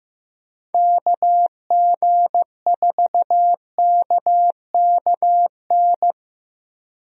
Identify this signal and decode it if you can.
Morse code, heard as a keyed tone, about 15 wpm, 700 Hz. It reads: KG4KKN
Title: USA Call Signs - 6 Characters - 15wpm